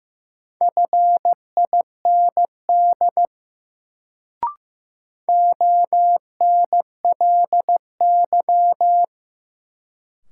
Morse code, heard as a keyed tone, about 15 wpm, 700 Hz. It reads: FIND ONLY
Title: Sets of 2 Words - New Word Drill - 15wpm